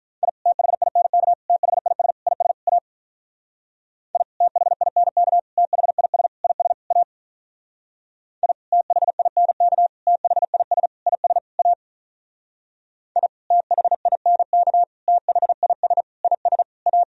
Morse code, heard as a keyed tone, about 43 wpm, 700 Hz. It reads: I THINK THIS IS A I THINK THIS IS A I THINK THIS IS A I THINK THIS IS A
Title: Speed-Racing - Sets of 5 Words 35wpm